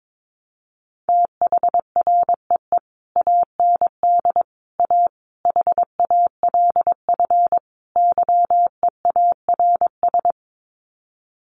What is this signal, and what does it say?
Morse code, heard as a keyed tone, about 22 wpm, 700 Hz. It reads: THREE AND A HALF YEARS
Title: Sets of 5 Words Encore2 22wpm